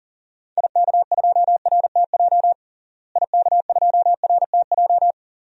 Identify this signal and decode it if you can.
Morse code, heard as a keyed tone, about 40 wpm, 700 Hz. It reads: IK2RTJ IK2RTJ
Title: DXCC-100 Call Signs - 6 Characters - Repeated 2x - 40wpm